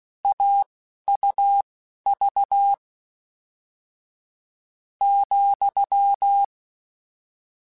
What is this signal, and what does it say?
Morse code, heard as a keyed tone, about 16 wpm, spaced as for 11 wpm, 800 Hz. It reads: AUV ,
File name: Difficult Combos / Difficult Combos-mp3s / AUV AUV4 @10